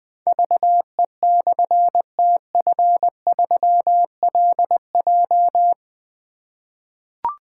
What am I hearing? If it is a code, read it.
Morse code, heard as a keyed tone, about 21 wpm, 700 Hz. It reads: VE/TF3LJ
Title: Difficult Call Signs 20wpm